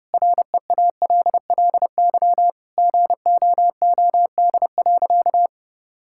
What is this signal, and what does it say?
Morse code, heard as a keyed tone, about 32 wpm, 700 Hz. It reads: REALLY GOOD.
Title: Sentences from Top 1000 Words - Review Part 6 - 30wpm